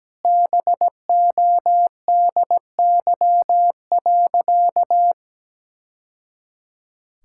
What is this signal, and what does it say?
Morse code, heard as a keyed tone, about 17 wpm, 700 Hz. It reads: BODY.
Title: Sentences from Top 1000 Words - Review Part 6 - 17wpm